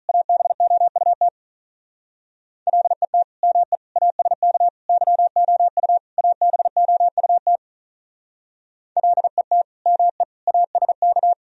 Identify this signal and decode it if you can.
Morse code, heard as a keyed tone, about 43 wpm, 700 Hz. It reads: ABOUT LET ME ASK YOU ABOUT LET ME ASK
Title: Speed-Racing - Sets of 5 Words 35wpm